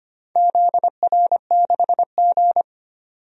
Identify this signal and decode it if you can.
Morse code, heard as a keyed tone, about 25 wpm, 700 Hz. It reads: ZR6G
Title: DXCC-100 Call Signs - 4 Characters - 25wpm